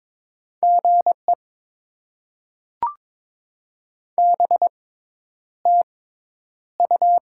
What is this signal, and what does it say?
Morse code, heard as a keyed tone, about 21 wpm, spaced as for 10 wpm, 700 Hz. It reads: GE BTU
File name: Top 20 QSO Elements - 6x Character Spacing - 22wpm